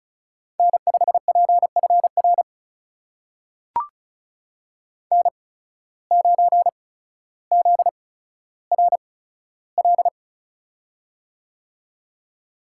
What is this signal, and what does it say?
Morse code, heard as a keyed tone, about 36 wpm, spaced as for 15 wpm, 700 Hz. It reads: N5PFR N9ZRL
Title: USA Call Signs - 5 Characters - 8x Character Spacing - 35wpm